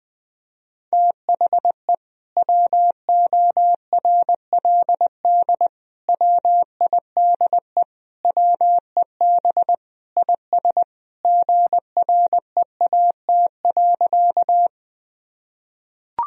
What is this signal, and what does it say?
Morse code, heard as a keyed tone, about 20 wpm, 700 Hz. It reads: THE WORLD WIDE WEB IS GREAT.
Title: Sentences from Top 1000 Words - Review Part 4 - 20wpm